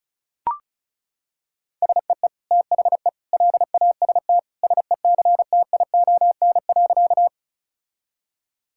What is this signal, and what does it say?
Morse code, heard as a keyed tone, about 34 wpm, 700 Hz. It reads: SEE THE LAST SECTION.
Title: Sentences from Top 1000 Words - Review Part 5 - 35wpm